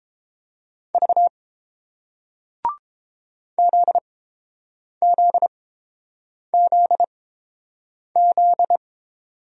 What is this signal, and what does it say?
Morse code, heard as a keyed tone, about 31 wpm, spaced as for 7 wpm, 700 Hz. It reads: V ZZZZ